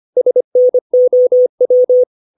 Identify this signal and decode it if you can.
Morse code, heard as a keyed tone, about 25 wpm, 500 Hz. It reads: SNOW
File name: 632 words / 632 Words 25WPM